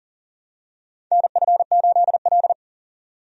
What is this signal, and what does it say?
Morse code, heard as a keyed tone, about 40 wpm, 700 Hz. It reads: NF8L
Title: USA Call Signs - 4 Characters - 40wpm